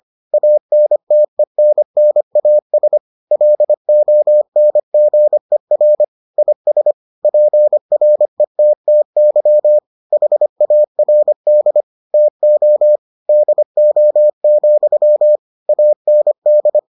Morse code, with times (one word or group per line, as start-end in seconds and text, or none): 0.3-3.0 ANTENNAS
3.3-6.0 LONGER
6.4-6.9 IS
7.2-9.8 PRETTY
10.1-11.8 HARD
12.1-13.0 TO
13.3-15.4 DO,
15.7-16.8 AND